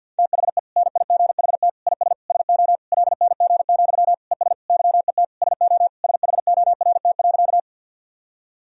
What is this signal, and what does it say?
Morse code, heard as a keyed tone, about 52 wpm, 700 Hz. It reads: THE NIGHT IS SO LNG, ES YET SO SHORT.
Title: Sentences from Top 500 Words and Abbreviations 50wpm